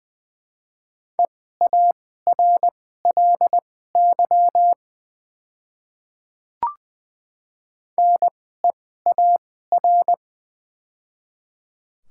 Morse code, heard as a keyed tone, about 20 wpm, spaced as for 14 wpm, 700 Hz. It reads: EARLY NEAR
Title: 2x Character Spacing - Top 500 Words - 20wpm